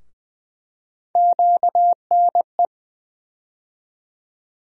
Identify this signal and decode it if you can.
Morse code, heard as a keyed tone, about 20 wpm, 700 Hz. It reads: QNE
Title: USA Call Sign Postfixes - 3 Letters 20wpm